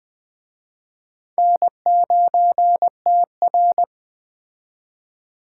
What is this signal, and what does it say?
Morse code, heard as a keyed tone, about 22 wpm, 700 Hz. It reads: N9TR